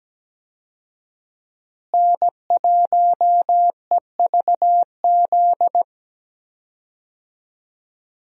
Morse code, heard as a keyed tone, about 17 wpm, 700 Hz. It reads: N1EVZ
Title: USA Call Signs - 5 Characters - Repeated 3x - 17wpm